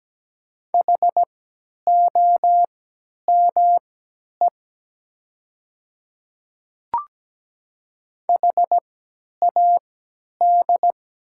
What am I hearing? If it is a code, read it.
Morse code, heard as a keyed tone, about 17 wpm, spaced as for 10 wpm, 700 Hz. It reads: HOME HAD
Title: 3x Character Spacing - Top 500 Words - 17wpm